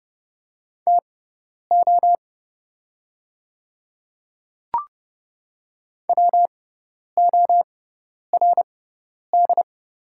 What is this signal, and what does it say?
Morse code, heard as a keyed tone, about 30 wpm, spaced as for 10 wpm, 690 Hz. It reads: TO WORD
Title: 6x Character Spacing - Top 200 Words - 30wpm